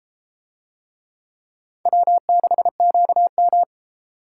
Morse code, heard as a keyed tone, about 31 wpm, 700 Hz. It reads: W6QM